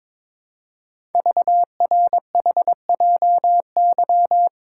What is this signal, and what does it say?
Morse code, heard as a keyed tone, about 22 wpm, 700 Hz. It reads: VRHJY
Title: Code Groups - 5 Letter:Numbers - 22wpm